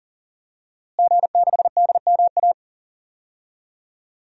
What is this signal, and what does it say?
Morse code, heard as a keyed tone, about 40 wpm, 700 Hz. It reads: G6DMA
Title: DXCC-100 Call Signs - 5 Characters - Repeated 3x - 40wpm